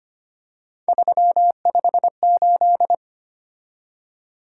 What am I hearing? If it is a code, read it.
Morse code, heard as a keyed tone, about 25 wpm, 700 Hz. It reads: 358
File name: Three Letter-Number Combinations - 25wpm